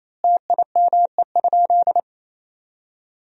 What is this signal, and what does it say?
Morse code, heard as a keyed tone, about 28 wpm, 700 Hz. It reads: TIME?
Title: Sentences from Top 1000 Words - Review Part 1 - 28wpm